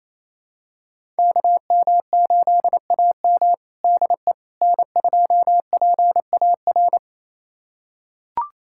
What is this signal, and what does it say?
Morse code, heard as a keyed tone, about 28 wpm, 700 Hz. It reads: KM8AM DE N2PAR
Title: USA Callsign DE Callsign - 5 Characters - 28wpm